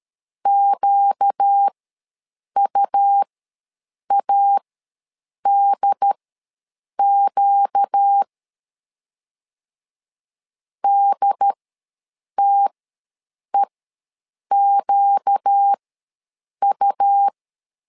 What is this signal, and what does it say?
Morse code, heard as a keyed tone, about 13 wpm, spaced as for 7 wpm, 800 Hz. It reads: QUADQ DTEQU
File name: LEZIONI / 4 Q ws 12wpm 1912